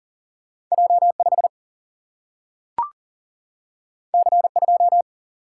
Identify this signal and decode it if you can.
Morse code, heard as a keyed tone, about 40 wpm, 700 Hz. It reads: J 5 C 2
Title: Single-Letter Single-Number 40wpm